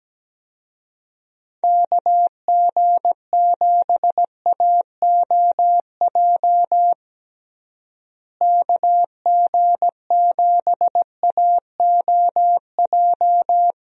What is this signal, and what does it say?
Morse code, heard as a keyed tone, about 17 wpm, 700 Hz. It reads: KG7AOJ KG7AOJ
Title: USA Call Signs - 6 Characters - Repeated 3x - 17wpm